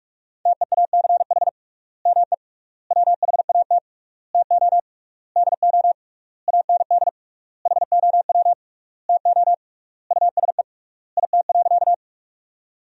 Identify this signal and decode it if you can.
Morse code, heard as a keyed tone, about 46 wpm, 700 Hz. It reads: TEACH ME WHAT TO DO AND HOW TO USE IT.
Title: 3x Word-Spacing - Sentences from Top 500 Words - 45wpm